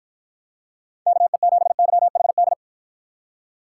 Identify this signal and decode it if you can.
Morse code, heard as a keyed tone, about 53 wpm, 700 Hz. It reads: KE7YHD